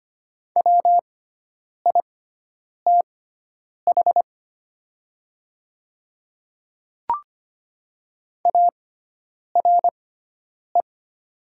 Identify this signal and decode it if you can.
Morse code, heard as a keyed tone, about 25 wpm, spaced as for 9 wpm, 700 Hz. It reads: WITH ARE
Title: 6x Character Spacing - Top 200 Words - 25wpm